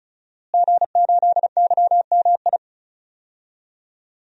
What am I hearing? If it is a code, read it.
Morse code, heard as a keyed tone, about 36 wpm, 700 Hz. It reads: G8YMI
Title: DXCC-100 Call Signs - 5 Characters - Repeated 3x - 35wpm